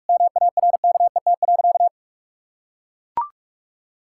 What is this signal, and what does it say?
Morse code, heard as a keyed tone, about 41 wpm, 700 Hz. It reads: MARKET.
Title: Sentences from Top 1000 Words - Review Part 6 - 45wpm